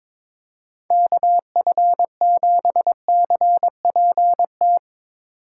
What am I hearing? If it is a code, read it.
Morse code, heard as a keyed tone, about 22 wpm, 700 Hz. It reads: KF7CPT
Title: Speed Racing - USA Call Signs 22wpm